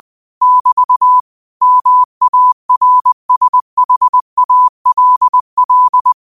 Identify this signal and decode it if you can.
Morse code, heard as a keyed tone, about 20 wpm, 1000 Hz. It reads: <BT> MARSHALL
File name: qotdfiles / 20wpm